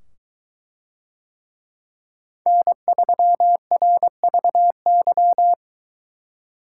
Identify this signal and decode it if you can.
Morse code, heard as a keyed tone, about 22 wpm, 700 Hz. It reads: N3RVY